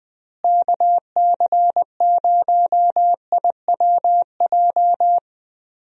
Morse code, heard as a keyed tone, about 21 wpm, 700 Hz. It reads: KC0IWJ
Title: USA Call Signs - 6 Characters - Repeated 3x - 20wpm